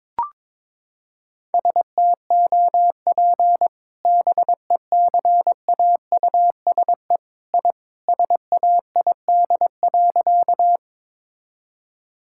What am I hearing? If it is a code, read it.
Morse code, heard as a keyed tone, about 22 wpm, 700 Hz. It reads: STOP BECAUSE I SAID.
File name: practice / 22wpm / Sentences with 2-Word Phrases - 22wpm